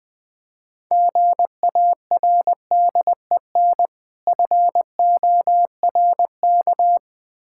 Morse code, heard as a keyed tone, about 21 wpm, 700 Hz. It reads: GARDEN FORK
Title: Difficult Words - Speed-Racing with Hints - 20wpm